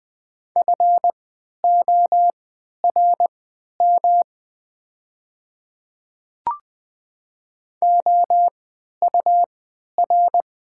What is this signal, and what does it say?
Morse code, heard as a keyed tone, about 20 wpm, spaced as for 11 wpm, 700 Hz. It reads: FORM OUR
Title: 3x Character Spacing - Top 200 Words - 20wpm